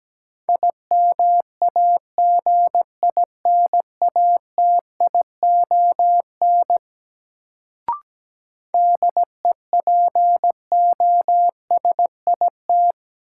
Morse code, heard as a keyed tone, about 17 wpm, 700 Hz. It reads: IMAGINATION DEPOSIT
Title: Sentences from Top 1300 Words - New Word Drill - 17wpm